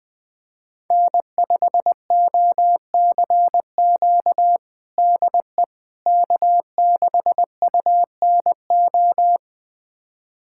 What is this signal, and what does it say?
Morse code, heard as a keyed tone, about 20 wpm, 700 Hz. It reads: N5OCQ DE K6UNO
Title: USA Callsign DE Callsign - 5 Characters - 20wpm